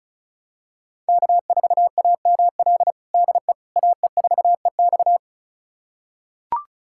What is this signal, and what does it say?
Morse code, heard as a keyed tone, about 36 wpm, 700 Hz. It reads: K4AML DE AE4EX
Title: USA Callsign DE Callsign - 5 Characters - 35wpm